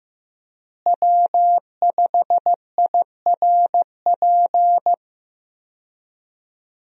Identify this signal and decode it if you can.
Morse code, heard as a keyed tone, about 15 wpm, 700 Hz. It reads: W5IRP
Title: USA Call Signs - 5 Characters - Repeated 3x - 15wpm